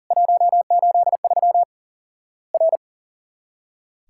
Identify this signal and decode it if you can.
Morse code, heard as a keyed tone, about 40 wpm, 700 Hz. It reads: 183 R
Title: Worked All Europe DX Contest - QTC Exchange - 40wpm